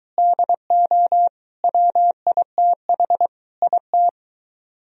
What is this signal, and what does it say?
Morse code, heard as a keyed tone, about 21 wpm, 700 Hz. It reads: DO WITH IT